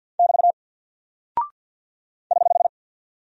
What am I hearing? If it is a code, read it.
Morse code, heard as a keyed tone, about 48 wpm, 700 Hz. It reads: <BT> <HH>
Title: Single Character (Letter:Number:Punctuation:Prosign) 50wpm